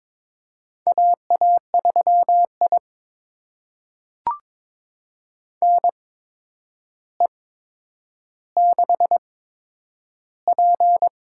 Morse code, heard as a keyed tone, about 22 wpm, spaced as for 9 wpm, 700 Hz. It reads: AA3I NE6P
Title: USA Call Signs - 4 Characters - 8x Character Spacing - 22wpm